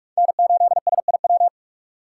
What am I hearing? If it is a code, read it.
Morse code, heard as a keyed tone, about 48 wpm, 700 Hz. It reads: N8SIW
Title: USA Call Signs - 5 Characters - 3x Character Spacing - 45wpm